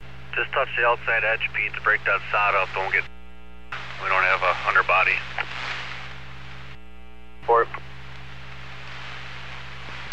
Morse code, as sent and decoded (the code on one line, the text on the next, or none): none